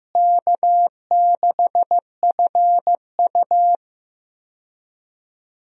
Morse code, heard as a keyed tone, about 15 wpm, 700 Hz. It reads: K6FU
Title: USA Call Signs - 4 Characters - Repeated 3x - 15wpm